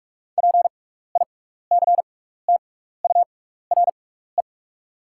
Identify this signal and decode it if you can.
Morse code, heard as a keyed tone, about 48 wpm, spaced as for 16 wpm, 700 Hz. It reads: PICTURE
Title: 6x Character Spacing - Top 500 Words - 45wpm